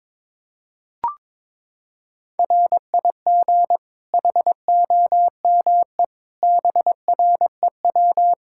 Morse code, heard as a keyed tone, about 22 wpm, 700 Hz. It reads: RIG HOME BREW
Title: K1USN SST QSO Practice 22wpm